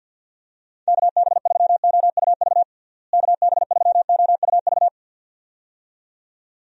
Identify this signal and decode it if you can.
Morse code, heard as a keyed tone, about 46 wpm, 700 Hz. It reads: KB3OUV KB3OUV
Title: USA Call Signs - 6 Characters - Repeated 2x - 50wpm